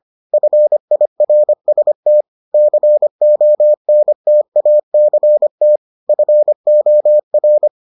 Morse code, heard as a keyed tone, about 25 wpm, 600 Hz. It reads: FIRST CONTACT FOR